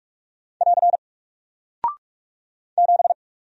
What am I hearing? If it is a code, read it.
Morse code, heard as a keyed tone, about 48 wpm, 700 Hz. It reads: <AR> 7
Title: Single Character (Letter:Number:Punctuation:Prosign) Alt 45wpm